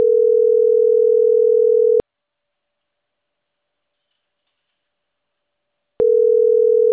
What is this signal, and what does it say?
Ringback tone (North America), 440 + 480 Hz.